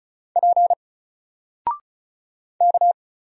Morse code, heard as a keyed tone, about 36 wpm, 700 Hz. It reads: P K
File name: Single-Letter-Number-35wpm